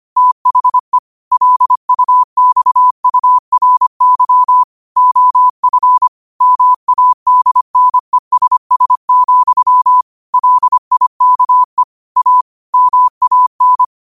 Morse code, heard as a keyed tone, about 25 wpm, 1000 Hz. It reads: THE LUXURY OF MADNESS, LIKE A MAN
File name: qotdfiles / 25wpm